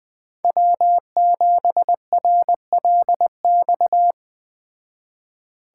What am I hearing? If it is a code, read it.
Morse code, heard as a keyed tone, about 20 wpm, 700 Hz. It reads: W7RLX
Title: Speed Racing - USA Call Signs 17wpm